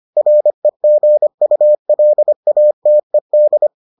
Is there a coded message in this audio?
Yes: Morse code from 0.2 to 3.7 s, heard as a keyed tone, about 25 wpm, 600 Hz. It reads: REGULATED